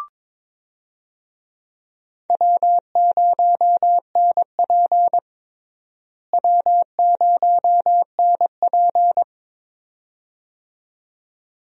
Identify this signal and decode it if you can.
Morse code, heard as a keyed tone, about 22 wpm, 700 Hz. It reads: W0NP W0NP
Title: USA Call Signs - 4 Characters - Repeated 2x - 22wpm